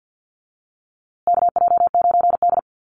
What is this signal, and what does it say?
Morse code, heard as a keyed tone, about 48 wpm, 700 Hz. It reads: KJ9D
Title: USA Call Signs Repeated 3x 50wpm